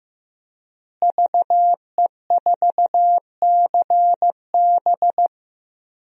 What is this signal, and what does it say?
Morse code, heard as a keyed tone, about 15 wpm, 700 Hz. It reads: VE4CB
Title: DXCC-100 Call Signs - 5 Characters - Repeated 3x - 15wpm